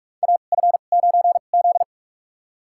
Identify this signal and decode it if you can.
Morse code, heard as a keyed tone, about 46 wpm, spaced as for 33 wpm, 700 Hz. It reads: AF9Z
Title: USA Call Signs - 4 Characters - 2x Character Spacing - 45wpm